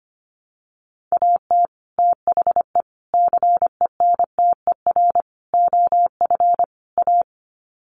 Morse code, heard as a keyed tone, about 25 wpm, 700 Hz. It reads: AT THE CENTER OF A
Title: Sets of 5 Words Encore2 25wpm